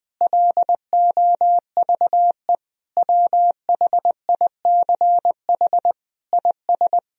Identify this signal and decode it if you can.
Morse code, heard as a keyed tone, about 20 wpm, 700 Hz. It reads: LOVE WHICH IS
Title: Sentences from Top 1000 Words - Review Part 1 - 20wpm